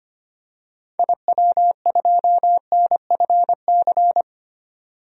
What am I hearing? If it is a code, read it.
Morse code, heard as a keyed tone, about 25 wpm, 700 Hz. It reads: IW2NFC
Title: DXCC-100 Call Signs - 6 Characters - 25wpm